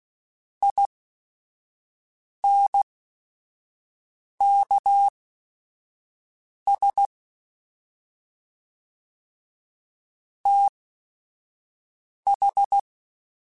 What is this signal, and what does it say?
Morse code, heard as a keyed tone, about 16 wpm, spaced as for 5 wpm, 800 Hz. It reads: INKS TH